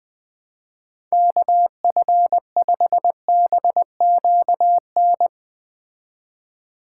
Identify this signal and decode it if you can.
Morse code, heard as a keyed tone, about 20 wpm, 700 Hz. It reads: KF5BQN
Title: Speed Racing - USA 2x3 Call Signs 17wpm